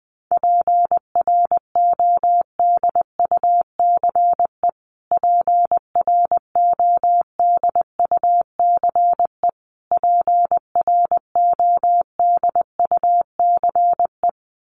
Morse code, heard as a keyed tone, about 20 wpm, 700 Hz. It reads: PRODUCE PRODUCE PRODUCE
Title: Top 500 Words Repeated 20wpm